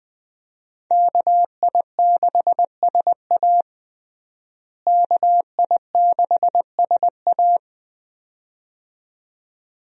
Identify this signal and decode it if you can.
Morse code, heard as a keyed tone, about 21 wpm, 700 Hz. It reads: KI6SA KI6SA
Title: USA Call Signs - 5 Characters - Repeated 2x - 20wpm